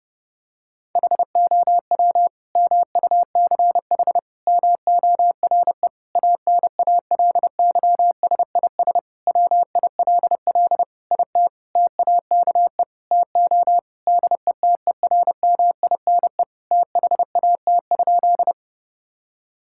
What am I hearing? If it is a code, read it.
Morse code, heard as a keyed tone, about 32 wpm, 700 Hz. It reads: HOW MUCH MORE ANALYSIS WILL IT TAKE TO DETERMINE THAT?
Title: Sentences from Top 1000 Words - Review Part 5 - 30wpm